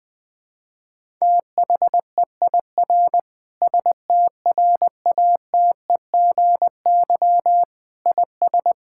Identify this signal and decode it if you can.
Morse code, heard as a keyed tone, about 21 wpm, 700 Hz. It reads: THEIR STRATEGY IS
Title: Sentences from Top 1000 Words - Review Part 5 - 20wpm